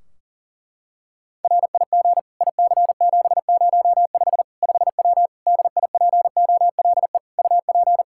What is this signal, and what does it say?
Morse code, heard as a keyed tone, about 40 wpm, 700 Hz. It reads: RIG IC705 5W DIPOLE UP
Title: K1USN SST QSO Practice 40wpm